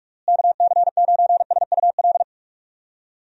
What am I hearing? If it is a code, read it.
Morse code, heard as a keyed tone, about 46 wpm, 700 Hz. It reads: KC9SUL
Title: USA Call Signs - 6 Characters - Repeated 2x - 45wpm